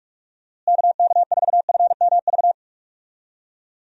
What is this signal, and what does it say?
Morse code, heard as a keyed tone, about 41 wpm, 700 Hz. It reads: KK4FMV
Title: USA Call Signs - 6 Characters - Repeated 2x - 45wpm